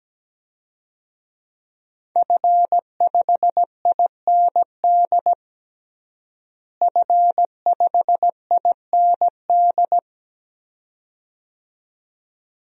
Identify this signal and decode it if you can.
Morse code, heard as a keyed tone, about 17 wpm, 700 Hz. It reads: F5IND F5IND
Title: DXCC-100 Call Signs - 5 Characters - Repeated 2x - 17wpm